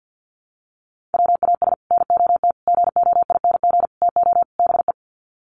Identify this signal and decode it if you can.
Morse code, heard as a keyed tone, about 50 wpm, 700 Hz. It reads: WAS NOT GOING TO BE
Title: Sets of 5 Words Encore 50wpm